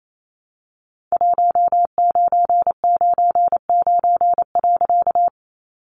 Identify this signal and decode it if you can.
Morse code, heard as a keyed tone, about 28 wpm, 700 Hz. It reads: 1999.
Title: A Sound Of Thunder 28wpm